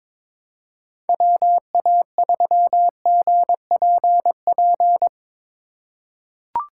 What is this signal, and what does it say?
Morse code, heard as a keyed tone, about 21 wpm, 700 Hz. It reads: WA3GPP
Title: K1USN SST QSO Practice - 22wpm